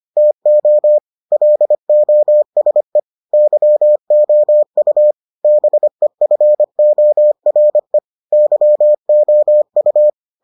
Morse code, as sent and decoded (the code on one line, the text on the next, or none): - --- / .-.. --- ... / -.-- --- ..- / -... ..-. --- .-. / -.-- --- ..-
TO LOSE YOU BEFORE YOU